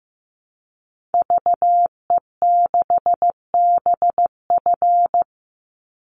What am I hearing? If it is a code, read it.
Morse code, heard as a keyed tone, about 15 wpm, 700 Hz. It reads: VE6BF
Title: Call Signs Repeated 2x - 15wpm